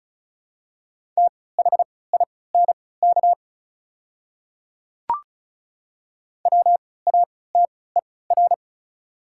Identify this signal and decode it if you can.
Morse code, heard as a keyed tone, about 36 wpm, spaced as for 20 wpm, 700 Hz. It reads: THINK WATER
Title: 3x Character Spacing - Top 200 Words - 35wpm